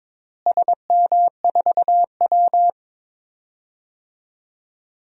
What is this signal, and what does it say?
Morse code, heard as a keyed tone, about 22 wpm, 700 Hz. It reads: SM4W
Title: DXCC-100 Call Signs - 4 Characters - Repeated 2x - 22wpm